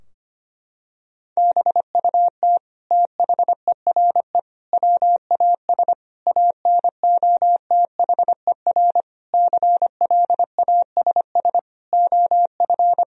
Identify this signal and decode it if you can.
Morse code, heard as a keyed tone, about 25 wpm, 700 Hz. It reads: BUT THERE WAS ANOTHER CLASS OF
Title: Sentences from Top 1000 Words - Review Part 2 - 25wpm